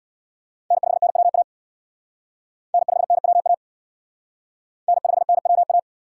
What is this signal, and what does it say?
Morse code, heard as a keyed tone, about 62 wpm, 700 Hz. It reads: N5NWA N5NWA N5NWA
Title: Speed Racing - USA Call Signs 50wpm